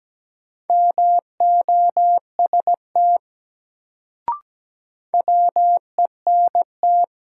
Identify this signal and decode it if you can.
Morse code, heard as a keyed tone, about 17 wpm, 700 Hz. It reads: MOST WENT
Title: Sentences from Top 200 Words - New Word Drill - 17wpm